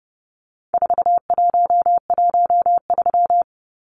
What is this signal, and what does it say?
Morse code, heard as a keyed tone, about 30 wpm, 700 Hz. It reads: 4113
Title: Numbers - 4-Digit 30wpm